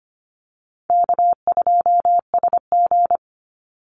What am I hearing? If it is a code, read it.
Morse code, heard as a keyed tone, about 25 wpm, 700 Hz. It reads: K2SG
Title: Call Signs Repeated 2x - 25wpm